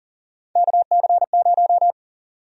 Morse code, heard as a keyed tone, about 40 wpm, 700 Hz. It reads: KC0
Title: USA Call Sign Prefixes - 40wpm